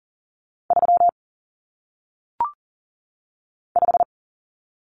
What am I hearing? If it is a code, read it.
Morse code, heard as a keyed tone, about 39 wpm, 700 Hz. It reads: 3 5
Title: Numbers - 1-Digit 40wpm